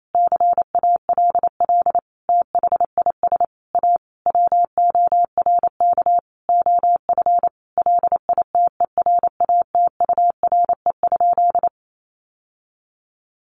Morse code, heard as a keyed tone, about 28 wpm, 700 Hz. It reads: CALL THIS A WORK OF LITERATURE?
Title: Sentences from Top 600 Words 28wpm